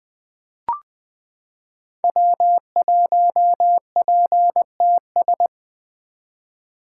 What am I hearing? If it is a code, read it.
Morse code, heard as a keyed tone, about 20 wpm, 700 Hz. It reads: W1PTS
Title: 2023 SOTA Activator Call Signs - 20wpm